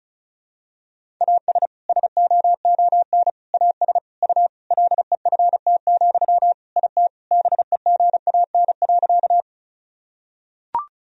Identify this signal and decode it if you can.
Morse code, heard as a keyed tone, about 36 wpm, 700 Hz. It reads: AS SOON AS U LEFT, IT BEGAN.
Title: Sentences from Top 400 Words and Abbreviations 35wpm